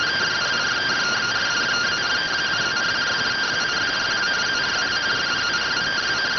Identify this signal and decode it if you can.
Two-tone teleprinter signal, about 170 Hz shift.